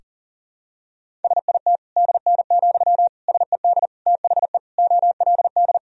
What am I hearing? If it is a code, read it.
Morse code, heard as a keyed tone, about 37 wpm, 700 Hz. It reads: SIT DN, SED THE OLD
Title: Sentences from Top 500 Words and Abbreviations 40wpm